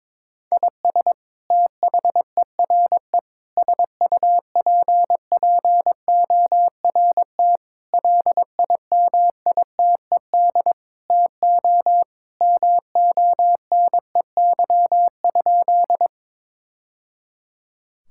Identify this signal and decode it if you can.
Morse code, heard as a keyed tone, about 22 wpm, 700 Hz. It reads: IS THERE SUPPORT LIMITED TO MONEY?
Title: Sentences from Top 1000 Words - Review Part 6 - 22wpm